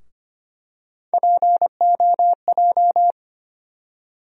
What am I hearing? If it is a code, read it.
Morse code, heard as a keyed tone, about 25 wpm, 700 Hz. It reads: POJ
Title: USA Call Sign Postfixes - 3 Letters 25wpm